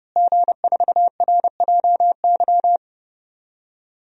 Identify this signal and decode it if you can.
Morse code, heard as a keyed tone, about 30 wpm, 700 Hz. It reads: G4RJY
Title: DXCC-100 Call Signs - 5 Characters - Repeated 3x - 30wpm